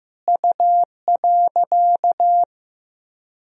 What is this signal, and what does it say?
Morse code, heard as a keyed tone, about 15 wpm, 700 Hz. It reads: U.
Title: Sentences from Top 100 Words and Abbreviations 15wpm